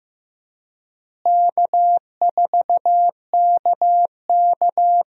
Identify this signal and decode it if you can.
Morse code, heard as a keyed tone, about 15 wpm, 700 Hz. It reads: K4KK
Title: USA Call Signs - 4 Characters - Repeated 3x - 15wpm